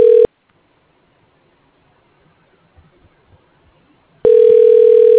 Ringback tone (North America), 440 + 480 Hz.